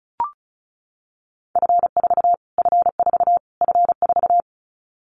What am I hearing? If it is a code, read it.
Morse code, heard as a keyed tone, about 34 wpm, 700 Hz. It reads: F4 F4 F4
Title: Callsign Prefixes for Top 25 DXCC - Repeated 3x - 35wpm